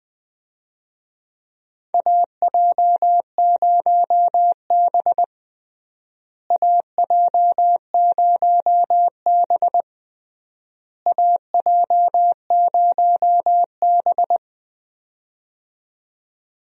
Morse code, heard as a keyed tone, about 21 wpm, 700 Hz. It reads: AJ0B AJ0B AJ0B
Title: USA Call Signs - 4 Characters - Repeated 3x - 20wpm